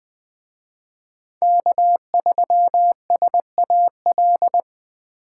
Morse code, heard as a keyed tone, about 20 wpm, 700 Hz. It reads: K3SAL
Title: Speed Racing - USA 1x3 Call Signs 17wpm